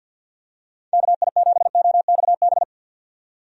Morse code, heard as a keyed tone, about 53 wpm, 700 Hz. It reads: KI7OXB
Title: USA Call Signs - 6 Characters - 50wpm